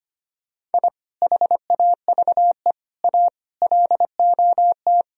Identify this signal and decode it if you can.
Morse code, heard as a keyed tone, about 31 wpm, 700 Hz. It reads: I HAVE A LOT
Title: Speed-Racing - Sets of 5 Words 25wpm